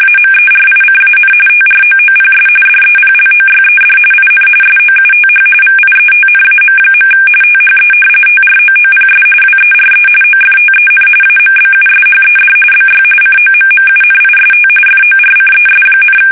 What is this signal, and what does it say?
RTTY, about 45.45 baud. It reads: AMERICAN TROOPS DEPLOYED IN SOUTH VIETNAM. THE TROOPS WERE OFTEN AMBUSHED BY COMMUNIST FORCES,
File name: fsk850 / battleshipnj